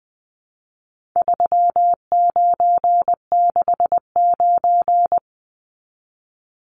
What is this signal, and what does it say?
Morse code, heard as a keyed tone, about 20 wpm, 700 Hz. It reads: 3969
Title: Numbers - 4-Digit 20wpm